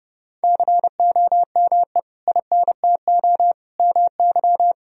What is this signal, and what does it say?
Morse code, heard as a keyed tone, about 32 wpm, 700 Hz. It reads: COME INTO MY
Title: Sentences from Top 1000 Words - Review Part 1 - 30wpm